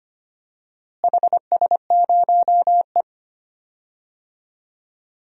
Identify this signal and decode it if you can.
Morse code, heard as a keyed tone, about 25 wpm, 700 Hz. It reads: HS0E
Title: DXCC-100 Call Signs - 4 Characters - Repeated 3x - 25wpm